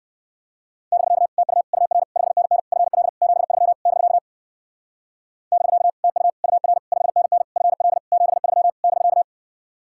Morse code, heard as a keyed tone, about 59 wpm, 700 Hz. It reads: <BK> TU UR 5NN FL 73 <BK> <BK> TU UR 5NN FL 73 <BK>